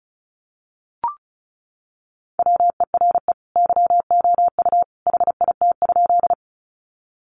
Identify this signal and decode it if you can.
Morse code, heard as a keyed tone, about 34 wpm, 700 Hz. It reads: WERE YOU HIT?
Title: Sentences from Top 1200 Words 35wpm